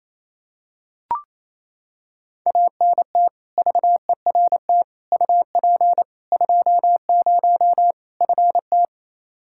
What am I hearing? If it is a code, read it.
Morse code, heard as a keyed tone, about 28 wpm, 700 Hz. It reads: ANT VERT UP 20 FT
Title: Practice QSO - Exchange 2 - Info - 28wpm